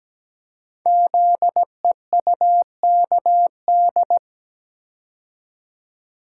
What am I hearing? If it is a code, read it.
Morse code, heard as a keyed tone, about 17 wpm, 700 Hz. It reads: ZEUKD
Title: Code Groups - 5 Letters - 17wpm